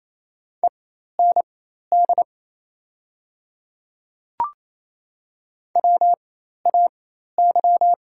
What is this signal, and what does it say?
Morse code, heard as a keyed tone, about 28 wpm, spaced as for 13 wpm, 700 Hz. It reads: END WAY
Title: 4x Character Spacing - Top 200 Words - 28wpm